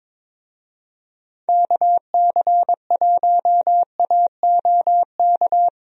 Morse code, heard as a keyed tone, about 22 wpm, 700 Hz. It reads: KC1AOK
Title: USA Call Signs - 6 Characters - Repeated 2x - 22wpm